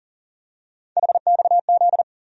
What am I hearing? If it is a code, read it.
Morse code, heard as a keyed tone, about 40 wpm, 700 Hz. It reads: HXZ
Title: USA Call Sign Postfixes - 3 Letters 40wpm